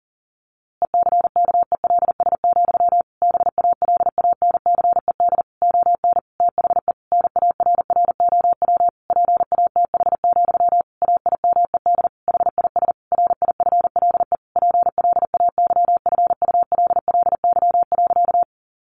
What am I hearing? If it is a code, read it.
Morse code, heard as a keyed tone, about 40 wpm, 700 Hz. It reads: ECKELS, BALANCED ON THE NARROW PATH, AIMED HIS RIFLE PLAYFULLY.
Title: A Sound Of Thunder 40wpm